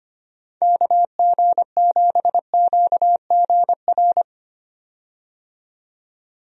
Morse code, heard as a keyed tone, about 25 wpm, 700 Hz. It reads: KG7QGR
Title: USA Call Signs - 6 Characters - 25wpm